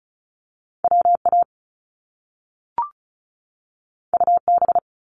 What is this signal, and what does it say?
Morse code, heard as a keyed tone, about 36 wpm, 700 Hz. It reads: WA UB
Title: Top Two Letter Combinations in Words 35wpm